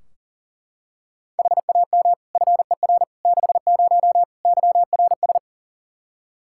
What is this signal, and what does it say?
Morse code, heard as a keyed tone, about 40 wpm, 700 Hz. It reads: HAM FER 60 YRS
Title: Practice QSO - Exchange 3 - More Info - 40wpm